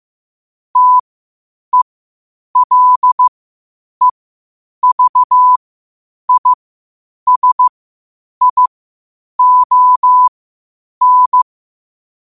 Morse code, heard as a keyed tone, about 15 wpm, spaced as for 8 wpm, 1000 Hz. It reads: TELEVISION
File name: 8wpm